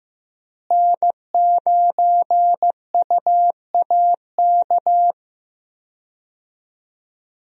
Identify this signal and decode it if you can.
Morse code, heard as a keyed tone, about 15 wpm, 700 Hz. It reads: N9UAK
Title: USA Call Signs - 5 Characters - Repeated 2x - 15wpm